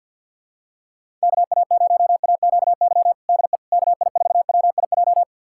Morse code, heard as a keyed tone, about 53 wpm, 700 Hz. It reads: KA0AQY DE KI4WIJ
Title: USA Callsign DE Callsign - 6 Characters - 50wpm